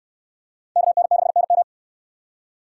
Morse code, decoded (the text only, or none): KN6NM